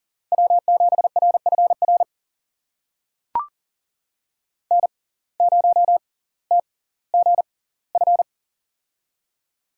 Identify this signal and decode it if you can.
Morse code, heard as a keyed tone, about 38 wpm, spaced as for 20 wpm, 700 Hz. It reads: W7RFR N0TGF
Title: USA Call Signs - 5 Characters - 6x Character Spacing - 40wpm